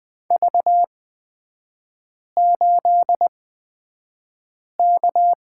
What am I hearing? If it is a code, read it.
Morse code, heard as a keyed tone, about 20 wpm, spaced as for 5 wpm, 700 Hz. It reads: V8K
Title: 255 Lesson 41e - Period - Three Characters 20wpm